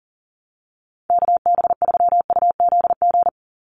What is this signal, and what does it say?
Morse code, heard as a keyed tone, about 40 wpm, 700 Hz. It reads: KB3UZG
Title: USA Call Signs 40wpm